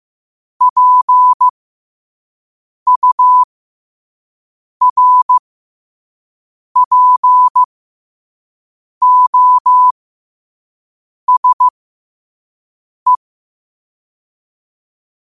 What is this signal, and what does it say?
Morse code, heard as a keyed tone, about 15 wpm, spaced as for 5 wpm, 1000 Hz. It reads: PURPOSE
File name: 2026-03-16-5wpm